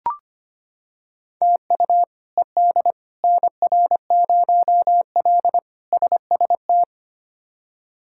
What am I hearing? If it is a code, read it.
Morse code, heard as a keyed tone, about 25 wpm, 700 Hz. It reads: TU ED NR0L SST
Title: K1USN SST QSO Practice - 25wpm